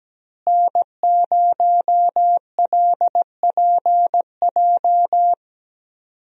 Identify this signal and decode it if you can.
Morse code, heard as a keyed tone, about 17 wpm, 700 Hz. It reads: N0LPJ
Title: USA Call Signs - 5 Characters - Repeated 2x - 17wpm